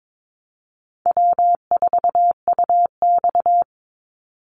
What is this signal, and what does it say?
Morse code, heard as a keyed tone, about 22 wpm, 700 Hz. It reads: W4UX
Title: Easy Call Signs - 22wpm